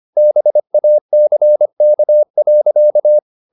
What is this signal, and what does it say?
Morse code, heard as a keyed tone, about 25 wpm, 600 Hz. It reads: BACK.
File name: the old man and the sea part 1 25wpm